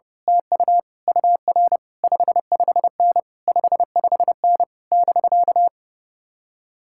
Morse code, heard as a keyed tone, about 28 wpm, 700 Hz. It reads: TU UR 55N 55N <BK>
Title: Speed-Racing Common QSO Phrases - 30wpm